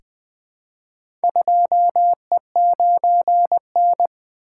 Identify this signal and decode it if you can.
Morse code, heard as a keyed tone, about 20 wpm, 700 Hz. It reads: 2E9N
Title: Four Letter-Number Combinations - 20wpm